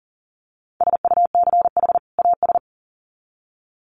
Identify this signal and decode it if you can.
Morse code, heard as a keyed tone, about 40 wpm, 700 Hz. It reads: SUCH AS
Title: Sets of 2 Words 40wpm